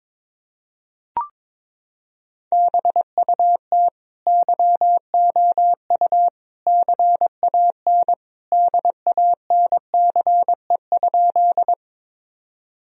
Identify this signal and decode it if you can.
Morse code, heard as a keyed tone, about 22 wpm, 700 Hz. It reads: BUT YOU CAN DANCE?
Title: Sentences from Top 1000 Words - Review Part 4 - 22wpm